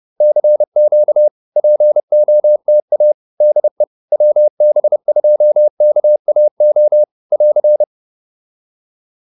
Morse code, heard as a keyed tone, about 30 wpm, 600 Hz. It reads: CQ POTA DE WB2KAO <AR>